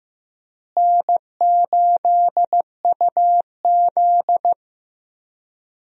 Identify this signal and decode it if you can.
Morse code, heard as a keyed tone, about 16 wpm, 700 Hz. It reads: N8UZ
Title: USA Call Signs - 4 Characters - Repeated 3x - 15wpm